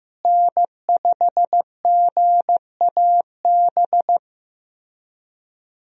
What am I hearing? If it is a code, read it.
Morse code, heard as a keyed tone, about 15 wpm, 700 Hz. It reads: N5GAB
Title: USA Call Signs - 5 Characters - Repeated 2x - 15wpm